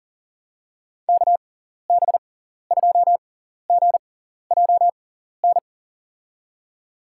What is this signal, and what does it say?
Morse code, heard as a keyed tone, about 40 wpm, spaced as for 14 wpm, 700 Hz. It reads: KB2GJN
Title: USA Call Signs - 6 Characters - 6x Character Spacing - 40wpm